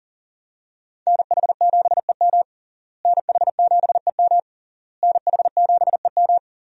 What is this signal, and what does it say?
Morse code, heard as a keyed tone, about 40 wpm, 700 Hz. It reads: NH7EM NH7EM NH7EM
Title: USA Call Signs - 5 Characters - Repeated 3x - 40wpm